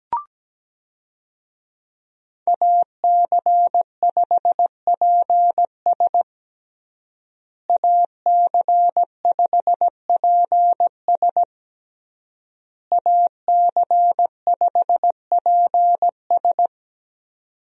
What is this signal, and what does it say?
Morse code, heard as a keyed tone, about 17 wpm, 700 Hz. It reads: AC5PS AC5PS AC5PS
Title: USA Call Signs - 5 Characters - Repeated 3x - 17wpm